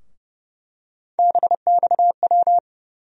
Morse code, heard as a keyed tone, about 30 wpm, 700 Hz. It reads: BXW
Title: USA Call Sign Postfixes - 3 Letters 30wpm